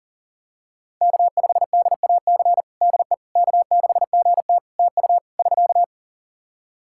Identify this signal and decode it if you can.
Morse code, heard as a keyed tone, about 37 wpm, 700 Hz. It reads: K5DAC DE K6GT TU <SK>
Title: Practice QSO - Exchange 4 - Ending - 40wpm